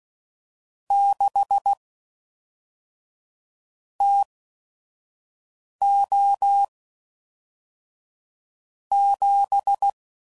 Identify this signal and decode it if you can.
Morse code, heard as a keyed tone, about 16 wpm, spaced as for 5 wpm, 800 Hz. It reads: 6 TO 7
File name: CW Code Groups @5WPM / Weather Sets / Weather 4 24